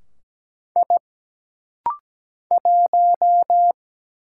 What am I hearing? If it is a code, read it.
Morse code, heard as a keyed tone, about 17 wpm, 700 Hz. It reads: I 1
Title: Single Letter-Number - Rapid-Fire 17wpm